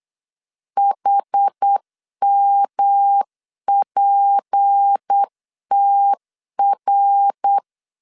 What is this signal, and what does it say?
Morse code, heard as a keyed tone, about 8 wpm, 800 Hz. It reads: HMPTR
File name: CW / DET / 1 6 1500C S C 8wpm 5219